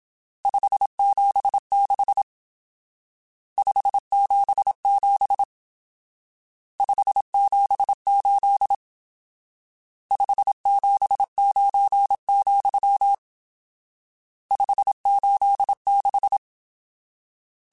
Morse code, heard as a keyed tone, about 27 wpm, 800 Hz. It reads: 576 577 578 579, 586